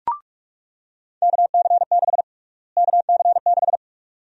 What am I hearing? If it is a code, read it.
Morse code, heard as a keyed tone, about 46 wpm, 700 Hz. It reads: KC6 KC6
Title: USA Call Sign Prefixes - Repeated 2x - 45wpm